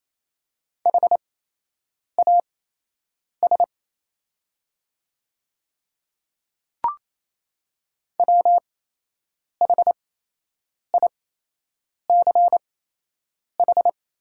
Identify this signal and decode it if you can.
Morse code, heard as a keyed tone, about 28 wpm, spaced as for 8 wpm, 700 Hz. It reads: HAS WHICH
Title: 8x Character Spacing - Top 100 Words - 28wpm